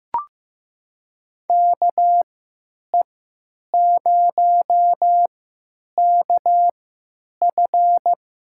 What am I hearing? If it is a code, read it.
Morse code, heard as a keyed tone, about 15 wpm, spaced as for 9 wpm, 700 Hz. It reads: KE0KF
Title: USA Call Signs - 6 Characters - 3x Character Spacing - 15wpm